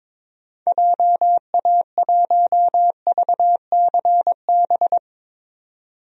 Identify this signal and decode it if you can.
Morse code, heard as a keyed tone, about 22 wpm, 700 Hz. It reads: JA1VCB
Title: DXCC-100 Call Signs - 6 Characters - 22wpm